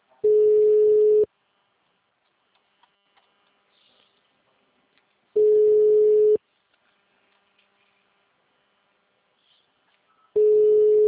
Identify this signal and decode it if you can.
Ringback tone (Europe, 425 Hz).
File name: out- -2051- - - .70158